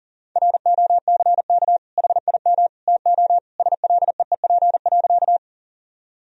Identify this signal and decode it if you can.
Morse code, heard as a keyed tone, about 40 wpm, 700 Hz. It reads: ROCK HIM TO SLEEP.
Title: Sentences from Top 1000 Words - Review Part 2 - 40wpm